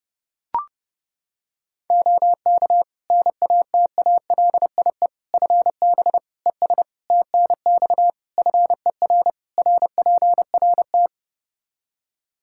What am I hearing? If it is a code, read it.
Morse code, heard as a keyed tone, about 30 wpm, 700 Hz. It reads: OK NATALIE FB ES TNX FER RPRT
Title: Practice QSO - Exchange 2 - Info - 30wpm